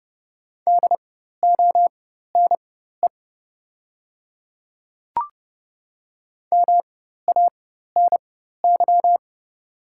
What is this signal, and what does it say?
Morse code, heard as a keyed tone, about 30 wpm, spaced as for 14 wpm, 690 Hz. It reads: DONE MANY
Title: 4x Character Spacing - Top 500 Words - 30wpm